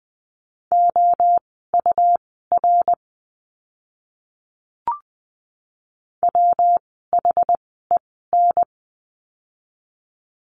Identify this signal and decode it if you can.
Morse code, heard as a keyed tone, about 20 wpm, spaced as for 14 wpm, 700 Hz. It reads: OUR WHEN
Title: 192 Lesson 31f - Letter v - Words 20wpm